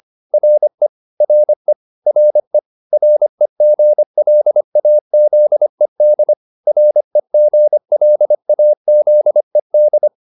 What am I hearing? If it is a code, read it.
Morse code, heard as a keyed tone, about 25 wpm, 600 Hz. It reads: RE RE RE REGLAZED REGLAZED